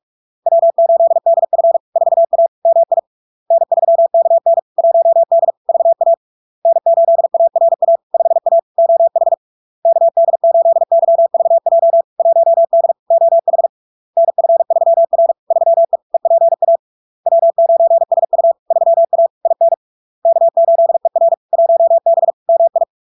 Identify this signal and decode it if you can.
Morse code, heard as a keyed tone, about 46 wpm, 670 Hz. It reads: W8DF 4A MI N3KN 1D VA N8ARA 5A OH KD8YVJ 1D OH NF3R 3E EPA W9SU 3A IN K8ER 1B MI
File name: LARK-FD-2022-Log-in-CW-45wpm-670-Hz